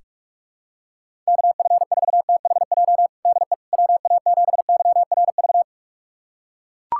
Morse code, heard as a keyed tone, about 42 wpm, 700 Hz. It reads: KF4THJ DE WA7YRV
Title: USA Callsign DE Callsign 45wpm